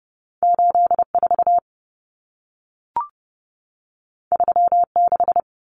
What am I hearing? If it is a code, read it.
Morse code, heard as a keyed tone, about 30 wpm, 700 Hz. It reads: 84 36
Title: Common Cut Numbers - 2-Digit 30wpm